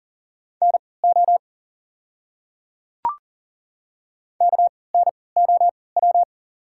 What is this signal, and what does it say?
Morse code, heard as a keyed tone, about 40 wpm, spaced as for 23 wpm, 690 Hz. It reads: NO KNOW
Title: 3x Character Spacing - Top 100 Words - 40wpm